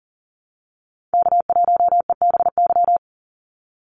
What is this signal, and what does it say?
Morse code, heard as a keyed tone, about 40 wpm, 700 Hz. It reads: K1EBY
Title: Call Signs - 40wpm